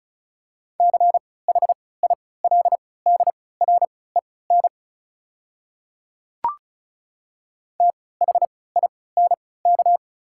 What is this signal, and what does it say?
Morse code, heard as a keyed tone, about 34 wpm, spaced as for 20 wpm, 690 Hz. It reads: CHILDREN THINK
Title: 3x Character Spacing - Top 500 Words - 35wpm